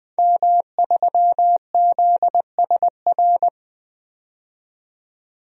Morse code, heard as a keyed tone, about 20 wpm, 700 Hz. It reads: M3ZSR
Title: DXCC-100 Call Signs - 5 Characters - 20wpm